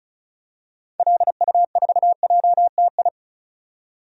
Morse code, heard as a keyed tone, about 36 wpm, 690 Hz. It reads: LU4JTI
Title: DXCC-100 Call Signs - 6 Characters - 35wpm